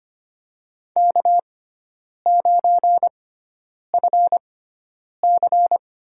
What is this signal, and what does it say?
Morse code, heard as a keyed tone, about 25 wpm, spaced as for 9 wpm, 700 Hz. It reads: K9FC
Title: USA Call Signs - 4 Characters - 6x Character Spacing - 25wpm